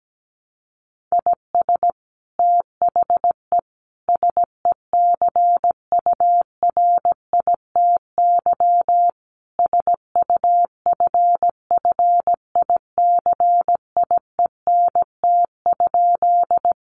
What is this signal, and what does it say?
Morse code, heard as a keyed tone, about 17 wpm, 700 Hz. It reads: IS THE SECURITY SUFFICIENT?
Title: Sentences from Top 1400 Words 17wpm